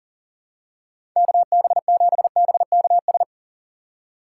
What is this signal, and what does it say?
Morse code, heard as a keyed tone, about 40 wpm, 700 Hz. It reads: KB7BKS
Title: USA Call Signs - 6 Characters - 3x Character Spacing - 40wpm